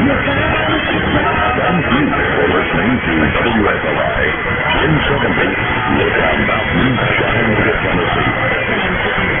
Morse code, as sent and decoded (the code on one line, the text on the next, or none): none